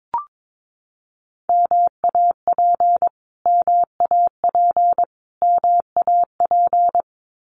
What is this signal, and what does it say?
Morse code, heard as a keyed tone, about 22 wpm, 700 Hz. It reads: MAP MAP MAP
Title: Top 500 Words Repeated 22wpm